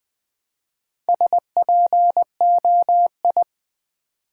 Morse code, heard as a keyed tone, about 20 wpm, 700 Hz. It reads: SPOI
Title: Build A Word - 1 Letter At A Time - 3 to 5-Letter Words - 20wpm